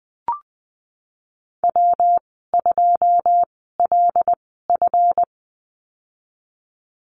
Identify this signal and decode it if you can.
Morse code, heard as a keyed tone, about 20 wpm, spaced as for 15 wpm, 700 Hz. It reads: W2LF
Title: 186 Lesson 30g - Letter b - Call Signs 20wpm